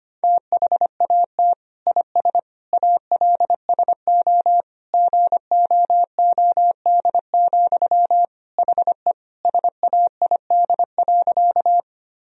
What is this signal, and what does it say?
Morse code, heard as a keyed tone, about 25 wpm, 700 Hz. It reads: THAT IS ALSO GOOD, HE SAID.
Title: Sentences from Top 1000 Words - Review Part 1 - 25wpm